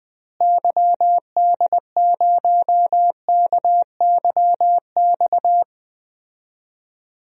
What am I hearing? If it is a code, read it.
Morse code, heard as a keyed tone, about 20 wpm, 700 Hz. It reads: YD0KYX
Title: DXCC-100 Call Signs - 6 Characters - Repeated 2x - 20wpm